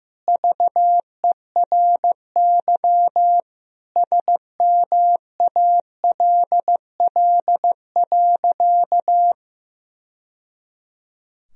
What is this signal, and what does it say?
Morse code, heard as a keyed tone, about 15 wpm, 700 Hz. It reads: VERY SMALL.
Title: Sentences from Top 1000 Words - Review Part 4 - 15wpm